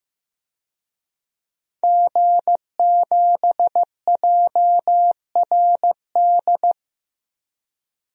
Morse code, heard as a keyed tone, about 15 wpm, 700 Hz. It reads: G7JRD
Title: DXCC-100 Call Signs - 5 Characters - Repeated 3x - 15wpm